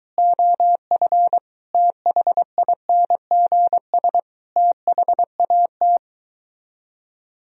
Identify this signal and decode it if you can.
Morse code, heard as a keyed tone, about 23 wpm, 700 Hz. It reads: OF THINGS THAT